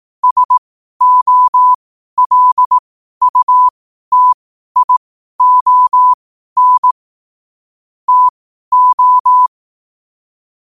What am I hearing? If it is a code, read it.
Morse code, heard as a keyed tone, about 18 wpm, spaced as for 12 wpm, 1000 Hz. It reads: SOLUTION TO
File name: qotdfiles / 12wpm